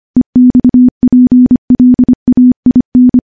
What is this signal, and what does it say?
Morse code, heard as a keyed tone, about 25 wpm, 250 Hz. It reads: EXPLAIN
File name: morse / morse-training-25WPM